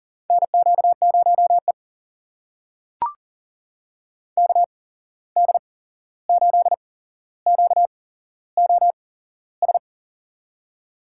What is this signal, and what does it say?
Morse code, heard as a keyed tone, about 40 wpm, spaced as for 15 wpm, 700 Hz. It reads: NQ0E KD8QOS
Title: USA Call Signs - 8x Character Spacing - 40wpm